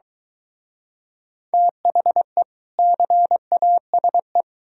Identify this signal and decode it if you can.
Morse code, heard as a keyed tone, about 22 wpm, 700 Hz. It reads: THE CASE